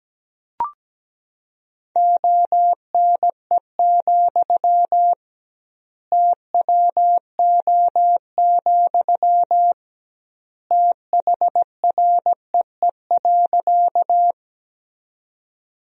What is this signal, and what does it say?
Morse code, heard as a keyed tone, about 17 wpm, 700 Hz. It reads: ONE, TWO, THREE.
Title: 2x Word-Spacing - Sentences from Top 200 Words - 17wpm